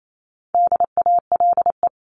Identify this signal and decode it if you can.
Morse code, heard as a keyed tone, about 28 wpm, 700 Hz. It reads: DALE
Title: CWT Exchange 28wpm